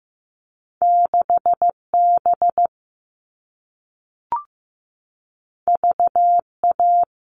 Morse code, heard as a keyed tone, about 15 wpm, 700 Hz. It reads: 6B VA
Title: All Cut Numbers - 2-Digit 15wpm